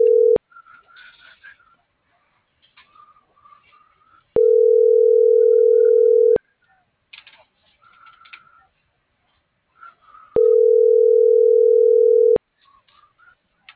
Ringback tone (North America), 440 + 480 Hz.